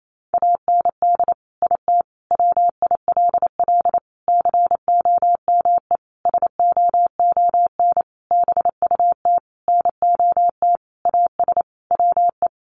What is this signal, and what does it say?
Morse code, heard as a keyed tone, about 28 wpm, 700 Hz. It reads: AND IT WILL COME SOON BUT NOT AS WE
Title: Sentences from Top 1100 Words 28wpm